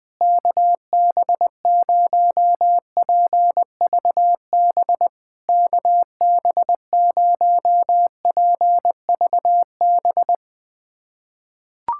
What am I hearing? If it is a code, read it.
Morse code, heard as a keyed tone, about 20 wpm, 700 Hz. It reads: KB0PVB KB0PVB
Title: POTA QSO Practice - 20wpm